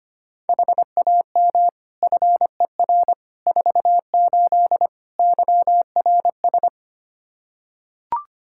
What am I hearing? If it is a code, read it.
Morse code, heard as a keyed tone, about 25 wpm, 700 Hz. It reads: HAM FER 48 YRS
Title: Practice QSO - Exchange 3 - More Info - 25wpm